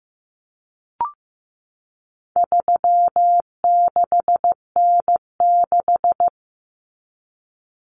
Morse code, heard as a keyed tone, about 15 wpm, 700 Hz. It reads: 36N6
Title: Common Cut Numbers - 4-Digit 15wpm